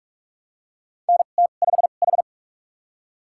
Morse code, heard as a keyed tone, about 44 wpm, spaced as for 32 wpm, 690 Hz. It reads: NT5H
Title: USA Call Signs - 4 Characters - 2x Character Spacing - 45wpm